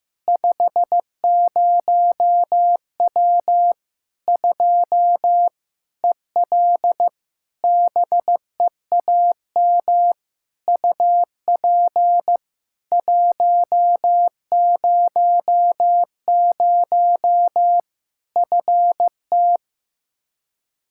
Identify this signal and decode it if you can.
Morse code, heard as a keyed tone, about 16 wpm, 700 Hz. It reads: 50W 2 EL BEAM UP 100 FT
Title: Practice QSO - Exchange 2 - Info - 15wpm